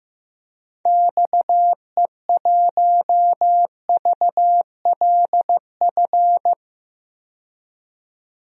Morse code, heard as a keyed tone, about 15 wpm, 700 Hz. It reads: XE1VLF
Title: DXCC-100 Call Signs - 6 Characters - 15wpm